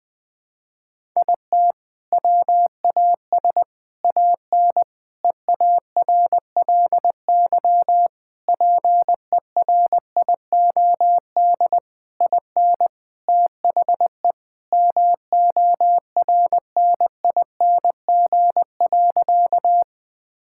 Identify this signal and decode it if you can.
Morse code, heard as a keyed tone, about 21 wpm, 700 Hz. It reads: IT WAS AN EARLY PERIOD IN THE MORNING.
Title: Sentences from Top 1000 Words - Review Part 5 - 20wpm